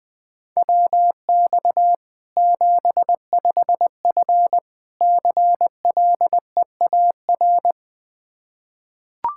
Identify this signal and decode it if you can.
Morse code, heard as a keyed tone, about 20 wpm, 700 Hz. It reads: WX 75F CLEAR
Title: Practice QSO - Exchange 2 - Info - 20wpm